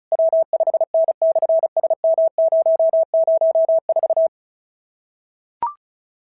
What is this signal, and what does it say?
Morse code, heard as a keyed tone, about 34 wpm, 650 Hz. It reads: W5N/SM004
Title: SOTA QSO Practice 35wpm